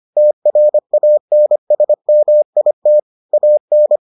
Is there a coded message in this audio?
Yes: Morse code from 0.2 to 4.0 s, heard as a keyed tone, about 25 wpm, 600 Hz. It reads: TRANSMIT AN